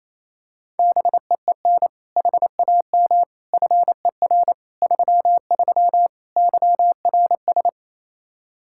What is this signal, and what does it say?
Morse code, heard as a keyed tone, about 27 wpm, 700 Hz. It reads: BEEN HAM FER 33 YRS
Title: Practice QSO - Exchange 3 - More Info - 28wpm